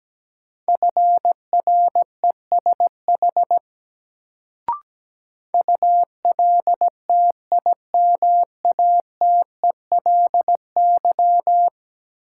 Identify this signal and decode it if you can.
Morse code, heard as a keyed tone, about 17 wpm, 700 Hz. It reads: FRESH ULTIMATELY
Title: Sentences from Top 1200 Words - New Word Drill - 17wpm